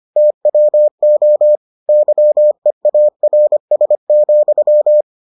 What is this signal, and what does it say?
Morse code, heard as a keyed tone, about 25 wpm, 600 Hz. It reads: TWO YEARS,